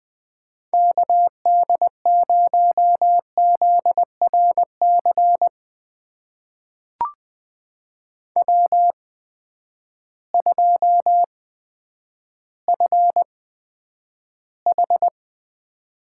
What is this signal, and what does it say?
Morse code, heard as a keyed tone, about 20 wpm, spaced as for 10 wpm, 700 Hz. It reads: KD0ZRC W2FH
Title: USA Call Signs - 8x Character Spacing - 20wpm